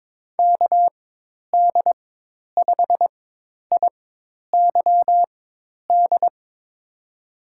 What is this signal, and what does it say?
Morse code, heard as a keyed tone, about 22 wpm, spaced as for 10 wpm, 700 Hz. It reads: KD5IYD
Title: USA Call Signs - 4x Character Spacing - 22wpm